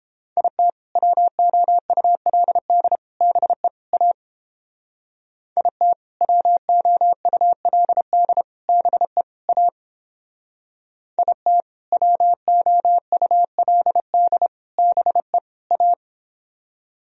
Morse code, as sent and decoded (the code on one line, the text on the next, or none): .. - / .-- --- ..- .-.. -.. / -... / .- / .. - / .-- --- ..- .-.. -.. / -... / .- / .. - / .-- --- ..- .-.. -.. / -... / .-
IT WOULD BE A IT WOULD BE A IT WOULD BE A